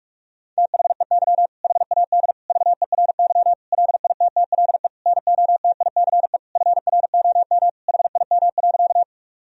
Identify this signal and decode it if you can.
Morse code, heard as a keyed tone, about 42 wpm, 700 Hz. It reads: THEY HAD VERY LITTLE NOTICE FROM HIM.
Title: Sentences from Top 1000 Words - Review Part 2 - 45wpm